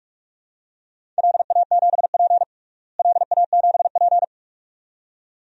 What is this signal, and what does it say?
Morse code, heard as a keyed tone, about 41 wpm, 700 Hz. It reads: LA7P LA7P
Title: DXCC-100 Call Signs - 4 Characters - Repeated 2x - 45wpm